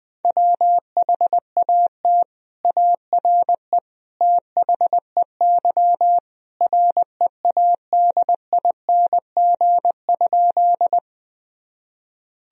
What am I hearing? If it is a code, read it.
Morse code, heard as a keyed tone, about 21 wpm, 700 Hz. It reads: WHAT ARE THEY READING?
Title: Sentences from Top 1000 Words - Review Part 3 - 20wpm